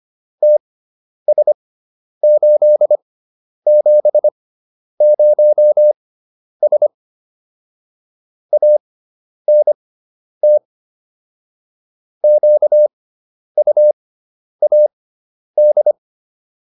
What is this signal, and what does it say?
Morse code, heard as a keyed tone, about 25 wpm, spaced as for 10 wpm, 600 Hz. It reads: TS870S ANT QUAD
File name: QSO201 10